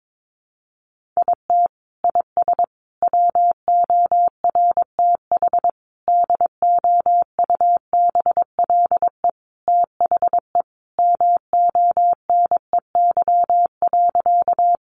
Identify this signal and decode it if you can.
Morse code, heard as a keyed tone, about 22 wpm, 700 Hz. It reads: IT IS WORTH DOUBLE THE MONEY.
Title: Sentences from Top 1400 Words 22wpm